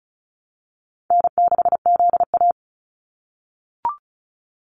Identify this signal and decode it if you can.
Morse code, heard as a keyed tone, about 36 wpm, 700 Hz. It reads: N6ZA
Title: NA SOTA Activator Callsigns 35wpm